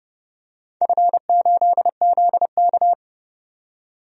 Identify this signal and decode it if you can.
Morse code, heard as a keyed tone, about 30 wpm, 700 Hz. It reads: F8ZK
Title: DXCC-100 Call Signs - 4 Characters - 30wpm